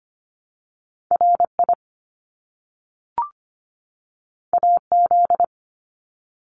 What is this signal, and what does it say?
Morse code, heard as a keyed tone, about 25 wpm, 700 Hz. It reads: RI AZ
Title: ARRL Sections 25wpm